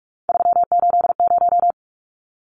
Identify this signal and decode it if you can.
Morse code, heard as a keyed tone, about 44 wpm, 700 Hz. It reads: 380
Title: Numbers - 3-Digit 45wpm